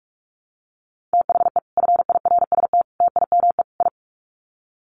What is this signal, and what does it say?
Morse code, heard as a keyed tone, about 44 wpm, 700 Hz. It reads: THE FIRST TIME I
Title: Sets of 4 Words 45wpm